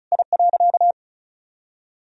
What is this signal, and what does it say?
Morse code, heard as a keyed tone, about 34 wpm, 700 Hz. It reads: I.
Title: Sentences from Top 1000 Words - Review Part 2 - 35wpm